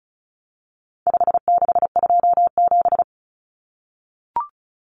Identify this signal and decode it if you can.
Morse code, heard as a keyed tone, about 34 wpm, 700 Hz. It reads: 5627
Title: Common Cut Numbers - 4-Digit 35wpm